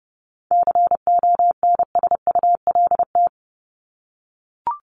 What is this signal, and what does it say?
Morse code, heard as a keyed tone, about 30 wpm, 700 Hz. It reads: CONSULT
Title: Top 1751-2000 Words 30wpm